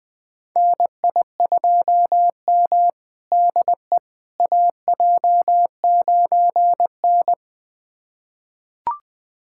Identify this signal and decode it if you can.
Morse code, heard as a keyed tone, about 20 wpm, 700 Hz. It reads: NI2M DE AJ9N
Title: USA Callsign DE Callsign - 4 Characters - 20wpm